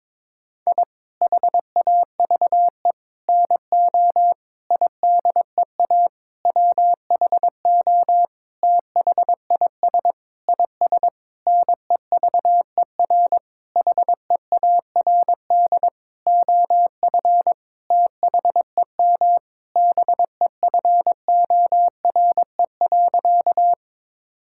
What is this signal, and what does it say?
Morse code, heard as a keyed tone, about 22 wpm, 700 Hz. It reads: I HAVE NO IDEA WHO THIS IS NEVER HEARD OF THEM BEFORE.
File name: Sentences with 9-Word Phrases - 22wpm